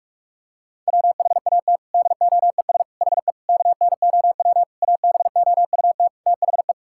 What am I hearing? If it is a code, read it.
Morse code, heard as a keyed tone, about 46 wpm, 700 Hz. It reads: WHAT DOES HE KNOW ABOUT THE
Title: Sentences from Top 1600 Words 45wpm